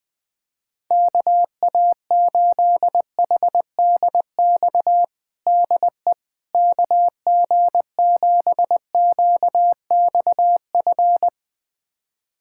Morse code, heard as a keyed tone, about 20 wpm, 700 Hz. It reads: KA8HDX DE KG7QXF
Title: USA Callsign DE Callsign - 6 Characters - 20wpm